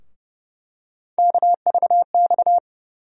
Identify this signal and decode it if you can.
Morse code, heard as a keyed tone, about 30 wpm, 700 Hz. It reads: KVX
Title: USA Call Sign Postfixes - 3 Letters 30wpm